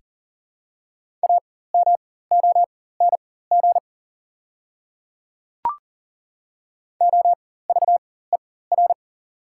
Morse code, heard as a keyed tone, about 39 wpm, spaced as for 18 wpm, 690 Hz. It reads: AMONG OVER
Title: 4x Character Spacing - Top 500 Words - 40wpm